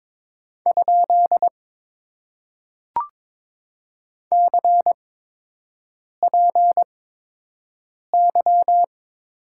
Morse code, heard as a keyed tone, about 22 wpm, spaced as for 6 wpm, 700 Hz. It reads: ? CPY
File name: Top 90 QSO Elements - 8x Character Spacing - 22wpm